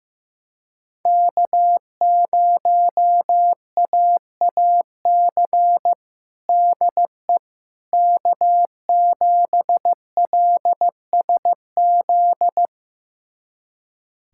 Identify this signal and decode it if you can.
Morse code, heard as a keyed tone, about 15 wpm, 700 Hz. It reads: K0AAC DE K7LSZ
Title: USA Callsign DE Callsign - 5 Characters - 15wpm